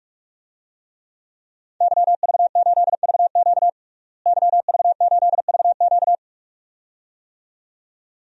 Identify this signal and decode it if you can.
Morse code, heard as a keyed tone, about 44 wpm, 700 Hz. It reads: YV8VQ YV8VQ
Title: DXCC-100 Call Signs - 5 Characters - Repeated 2x - 45wpm